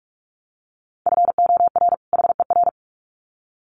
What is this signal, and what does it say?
Morse code, heard as a keyed tone, about 44 wpm, 700 Hz. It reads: FOR HER
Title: Sets of 2 Words 45wpm